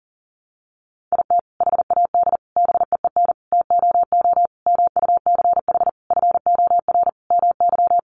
Morse code, heard as a keyed tone, about 40 wpm, 700 Hz. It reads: IT HAD BEEN TOO MUCH FOR MY
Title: Sentences from Top 1300 Words 40wpm